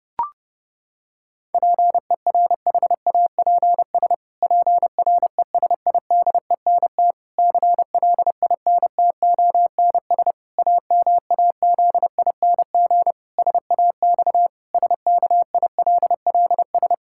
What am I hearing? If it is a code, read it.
Morse code, heard as a keyed tone, about 30 wpm, 700 Hz. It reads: PERHAPS PRESIDENT CLINTONS AMAZING SAX SKILLS
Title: Pangram Challenge v2 - 30wpm